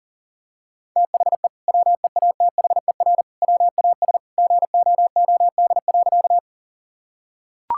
Morse code, heard as a keyed tone, about 40 wpm, 700 Hz. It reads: THE WEATHER WAS GOOD.
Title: Sentences from Top 1000 Words - Review Part 5 - 40wpm